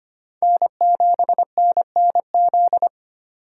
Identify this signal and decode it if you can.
Morse code, heard as a keyed tone, about 25 wpm, 700 Hz. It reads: N7NNZ
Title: USA Call Signs - 3x Character Spacing - 25wpm